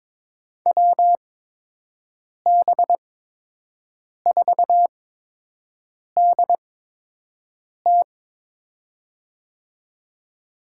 Morse code, heard as a keyed tone, about 22 wpm, spaced as for 6 wpm, 700 Hz. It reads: WB4DT
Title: USA Call Signs - 8x Character Spacing - 22wpm